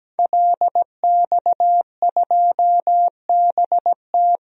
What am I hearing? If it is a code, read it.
Morse code, heard as a keyed tone, about 17 wpm, 700 Hz. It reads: LX2BT
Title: DXCC-100 Call Signs - 5 Characters - Repeated 2x - 17wpm